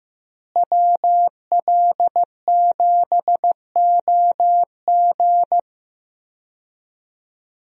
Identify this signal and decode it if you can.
Morse code, heard as a keyed tone, about 16 wpm, 700 Hz. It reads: WL7OG
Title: USA Call Signs - 5 Characters - Repeated 2x - 15wpm